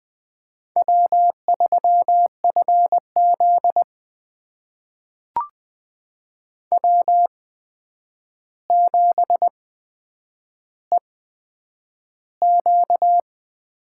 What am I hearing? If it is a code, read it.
Morse code, heard as a keyed tone, about 20 wpm, spaced as for 9 wpm, 700 Hz. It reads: W3FZ W7EQ
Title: USA Call Signs - 4 Characters - 8x Character Spacing - 20wpm